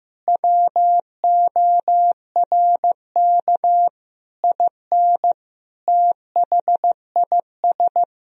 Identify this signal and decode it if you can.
Morse code, heard as a keyed tone, about 16 wpm, 700 Hz. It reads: WORK IN THIS
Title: Sentences from Top 1000 Words - Review Part 3 - 15wpm